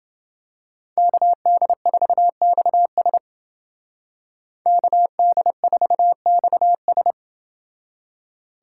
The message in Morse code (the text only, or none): KD4XS KD4XS